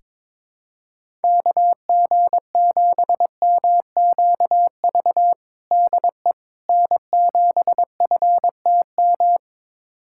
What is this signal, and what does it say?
Morse code, heard as a keyed tone, about 22 wpm, 700 Hz. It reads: KG7MQV DE N7FTM
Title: USA Callsign DE Callsign 22wpm